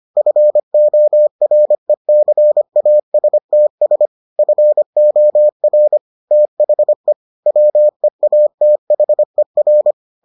Morse code, heard as a keyed tone, about 25 wpm, 600 Hz. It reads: FORECASTS FOR THE WEATHER